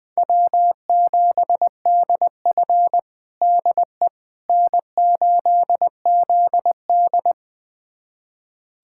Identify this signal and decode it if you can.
Morse code, heard as a keyed tone, about 20 wpm, 700 Hz. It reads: W7DF DE N8ZD
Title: USA Callsign DE Callsign - 4 Characters - 20wpm